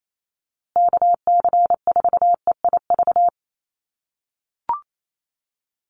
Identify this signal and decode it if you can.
Morse code, heard as a keyed tone, about 28 wpm, 700 Hz. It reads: KC4EIV
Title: USA Call Signs Repeated 3x 28wpm